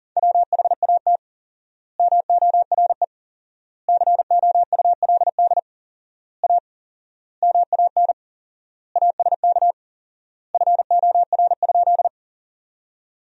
Morse code, heard as a keyed tone, about 40 wpm, 700 Hz. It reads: WHAT MORE COULD A MAN ASK FOR?
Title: 4x Word-Spacing - Sentences from Top 200 Words - 40wpm